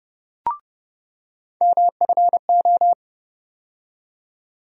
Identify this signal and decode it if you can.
Morse code, heard as a keyed tone, about 30 wpm, 700 Hz. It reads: MFO
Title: USA Call Sign Postfixes - 3 Letters 30wpm